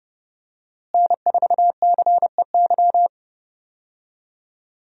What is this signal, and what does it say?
Morse code, heard as a keyed tone, about 30 wpm, 700 Hz. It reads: N4CEY
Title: USA Call Signs - 5 Characters - Repeated 2x - 30wpm